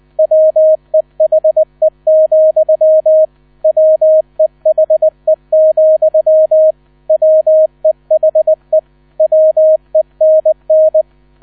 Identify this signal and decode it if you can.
Morse code, heard as a keyed tone, about 19 wpm, 630 Hz. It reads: WEHE, WEHE, WEHE WENN
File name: 000 Vorwort 20wpm